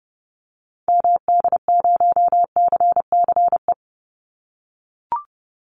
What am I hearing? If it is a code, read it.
Morse code, heard as a keyed tone, about 30 wpm, 700 Hz. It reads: MD0CCE
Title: Call Signs - 30wpm